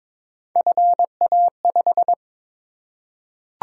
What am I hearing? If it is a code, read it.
Morse code, heard as a keyed tone, about 22 wpm, 700 Hz. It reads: FA5
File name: practice / 22wpm / Three Letter-Number Combinations - 22wpm